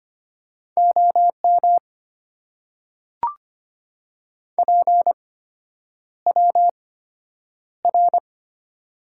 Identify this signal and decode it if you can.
Morse code, heard as a keyed tone, about 25 wpm, spaced as for 9 wpm, 700 Hz. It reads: OM PWR
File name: Top 90 QSO Elements - 8x Character Spacing - 25wpm